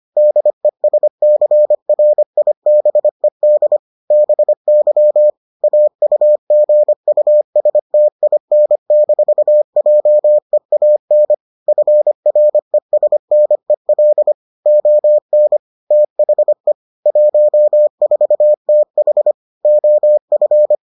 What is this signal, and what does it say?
Morse code, heard as a keyed tone, about 25 wpm, 600 Hz. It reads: DESCRIBED BY AUGUSTIN-JEAN FRESNEL ON THE 14TH OF